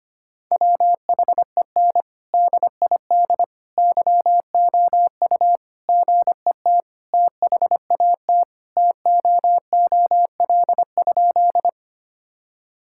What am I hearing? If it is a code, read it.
Morse code, heard as a keyed tone, about 25 wpm, 700 Hz. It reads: WHEN DID YOU GET THAT TOOL?
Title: Sentences from Top 1000 Words - Review Part 6 - 25wpm